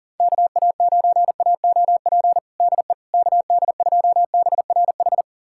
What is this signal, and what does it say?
Morse code, heard as a keyed tone, about 37 wpm, 700 Hz. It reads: KA9AOP DE KD2BRH
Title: USA Callsign DE Callsign - 6 Characters - 40wpm